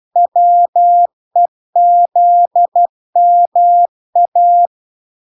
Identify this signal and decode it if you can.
Morse code, heard as a keyed tone, about 12 wpm, 700 Hz. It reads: WEZMA